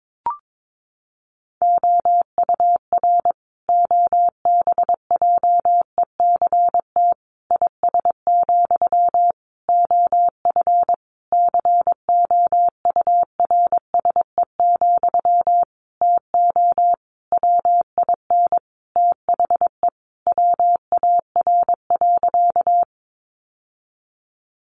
Morse code, heard as a keyed tone, about 22 wpm, 700 Hz. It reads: OUR OBJECT IS, OF COURSE, TO WIN THE WAR.
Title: Sentences from Top 1500 Words 22wpm